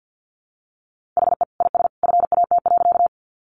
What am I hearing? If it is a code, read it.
Morse code, heard as a keyed tone, about 50 wpm, 700 Hz. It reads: HE IS FAT.
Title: Easy Phrases 50wpm